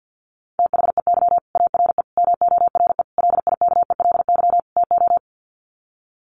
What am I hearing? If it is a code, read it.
Morse code, heard as a keyed tone, about 50 wpm, 700 Hz. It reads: THEY ARE MORE LIKELY TO
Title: Sets of 5 Words Encore2 50wpm